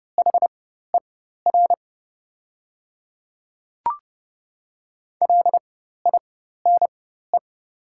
Morse code, heard as a keyed tone, about 30 wpm, spaced as for 14 wpm, 690 Hz. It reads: HER LINE
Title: 4x Character Spacing - Top 200 Words - 30wpm